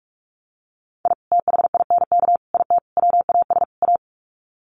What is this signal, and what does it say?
Morse code, heard as a keyed tone, about 44 wpm, 700 Hz. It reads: I THINK IT WAS A
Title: Sets of 5 Words 45wpm